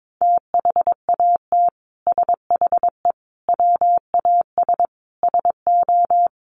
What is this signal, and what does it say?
Morse code, heard as a keyed tone, about 22 wpm, 700 Hz. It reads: THAT SHE WAS SO
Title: Sentences from Top 1400 Words 22wpm